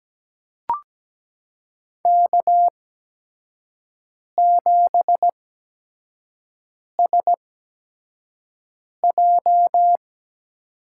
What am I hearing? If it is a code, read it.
Morse code, heard as a keyed tone, about 17 wpm, spaced as for 5 wpm, 700 Hz. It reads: K7SJ
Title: USA Call Signs - 8x Character Spacing - 17wpm